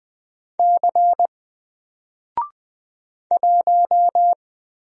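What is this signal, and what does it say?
Morse code, heard as a keyed tone, about 20 wpm, 700 Hz. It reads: C 1
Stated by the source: Single letter and number Morse drill at 20 wpm, from N4LCD.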